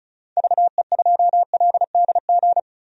Morse code, heard as a keyed tone, about 34 wpm, 700 Hz. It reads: VE2LDG
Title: DXCC-100 Call Signs - 6 Characters - 35wpm